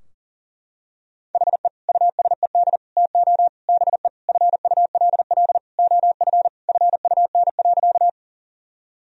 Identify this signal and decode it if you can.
Morse code, heard as a keyed tone, about 40 wpm, 700 Hz. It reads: HE USED TO BE FULL OF FUN.
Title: Sentences from Top 1000 Words - Review Part 4 - 40wpm